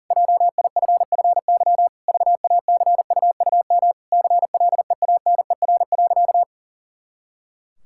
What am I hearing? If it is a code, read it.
Morse code, heard as a keyed tone, about 40 wpm, 700 Hz. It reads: JIFFY VACUUM CLEANER.
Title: Pangram Challenge v2 - 40wpm